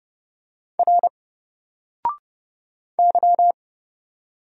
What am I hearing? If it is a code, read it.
Morse code, heard as a keyed tone, about 30 wpm, 700 Hz. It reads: R Y
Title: Single Letter-Number Alt 30wpm